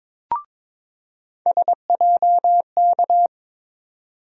Morse code, heard as a keyed tone, about 21 wpm, 700 Hz. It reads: SJK
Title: USA Call Sign Postfixes - 3 Letters 22wpm